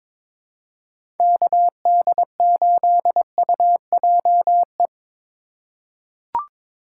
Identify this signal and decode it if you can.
Morse code, heard as a keyed tone, about 22 wpm, 700 Hz. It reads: KD8UJE
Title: USA Call Signs - 6 Characters - 3x Character Spacing - 22wpm